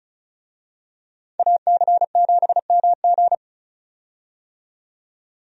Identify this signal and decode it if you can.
Morse code, heard as a keyed tone, about 36 wpm, 700 Hz. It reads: AC7MG
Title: USA Call Signs - 5 Characters - 35wpm